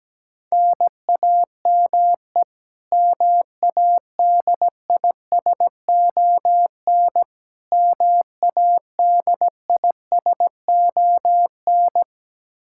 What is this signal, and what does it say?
Morse code, heard as a keyed tone, about 17 wpm, 700 Hz. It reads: NAME MADISON MADISON
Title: K1USN SST QSO Practice 17wpm